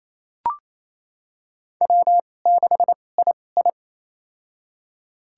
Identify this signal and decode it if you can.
Morse code, heard as a keyed tone, about 28 wpm, spaced as for 20 wpm, 700 Hz. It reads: W6II
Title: USA Call Signs - 4 Characters - 2x Character Spacing - 28wpm